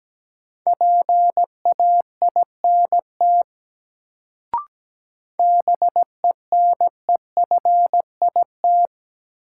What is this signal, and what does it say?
Morse code, heard as a keyed tone, about 17 wpm, 700 Hz. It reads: PAINT BENEFIT
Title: Sentences from Top 1000 Words - Review Part 6 - New Word Drill - 17wpm